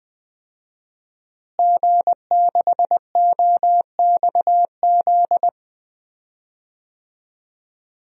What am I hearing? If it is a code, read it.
Morse code, heard as a keyed tone, about 21 wpm, 700 Hz. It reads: G6OXZ
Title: DXCC-100 Call Signs - 5 Characters - Repeated 3x - 20wpm